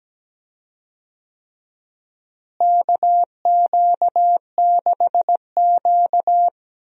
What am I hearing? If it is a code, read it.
Morse code, heard as a keyed tone, about 17 wpm, 700 Hz. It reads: KQ6Q
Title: USA Call Signs - 4 Characters - Repeated 3x - 17wpm